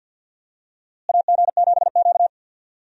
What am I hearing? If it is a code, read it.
Morse code, heard as a keyed tone, about 48 wpm, 700 Hz. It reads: AG7Q
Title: USA Call Signs - 4 Characters - Repeated 2x - 50wpm